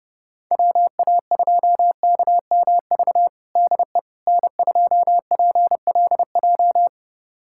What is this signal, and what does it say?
Morse code, heard as a keyed tone, about 30 wpm, 700 Hz. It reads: WA2KMV DE N2PLJ
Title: USA Callsign DE Callsign 30wpm